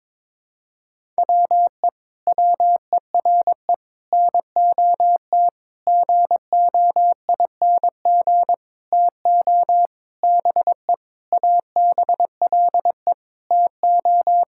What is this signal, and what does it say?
Morse code, heard as a keyed tone, about 22 wpm, 700 Hz. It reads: WE WERE NOT GOING TO BE ABLE TO
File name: Sentences with 7-Word Phrases - 22wpm